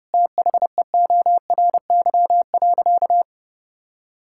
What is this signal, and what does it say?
Morse code, heard as a keyed tone, about 30 wpm, 700 Hz. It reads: THEORY.
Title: Sentences from Top 1000 Words - Review Part 5 - 30wpm